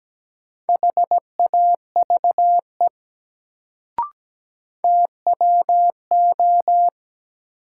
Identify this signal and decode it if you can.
Morse code, heard as a keyed tone, about 17 wpm, 700 Hz. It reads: HAVE TWO
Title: Sentences from Top 100 Words - New Word Drill - 17wpm